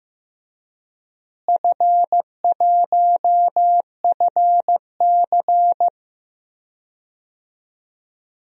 Morse code, heard as a keyed tone, about 15 wpm, 700 Hz. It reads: F1FC
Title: DXCC-100 Call Signs - 4 Characters - Repeated 2x - 15wpm